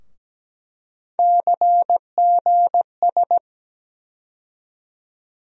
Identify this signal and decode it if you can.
Morse code, heard as a keyed tone, about 17 wpm, 700 Hz. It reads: CGS
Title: USA Call Sign Postfixes - 3 Letters 17wpm